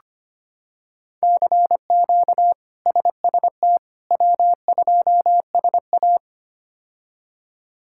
Morse code, heard as a keyed tone, about 25 wpm, 700 Hz. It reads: CQ SST W2SA
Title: K1USN SST QSO Practice - 25wpm